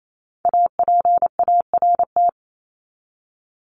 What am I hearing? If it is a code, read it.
Morse code, heard as a keyed tone, about 28 wpm, 700 Hz. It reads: APART
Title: Top 1501-1750 Words 28wpm